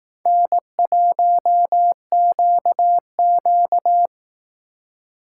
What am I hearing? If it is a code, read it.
Morse code, heard as a keyed tone, about 18 wpm, 700 Hz. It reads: N1QQ
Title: Speed Racing - USA 1x2 Call Signs 15wpm